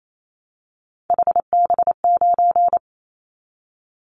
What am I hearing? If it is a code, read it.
Morse code, heard as a keyed tone, about 28 wpm, 700 Hz. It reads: HB9
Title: Callsign Prefixes for Top 25 DXCC - Repeated 3x - 28wpm